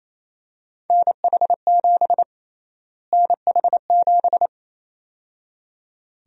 Morse code, heard as a keyed tone, about 28 wpm, 700 Hz. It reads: NH7 NH7
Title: USA Call Sign Prefixes - Repeated 2x - 28wpm